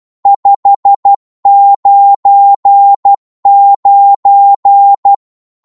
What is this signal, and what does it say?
Morse code, heard as a keyed tone, about 12 wpm, 800 Hz. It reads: 599
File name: Morse / qso-12wpm